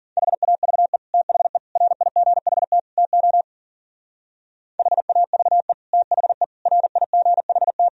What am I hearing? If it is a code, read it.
Morse code, heard as a keyed tone, about 52 wpm, 700 Hz. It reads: HAVE THE RIGHT TO HAVE THE RIGHT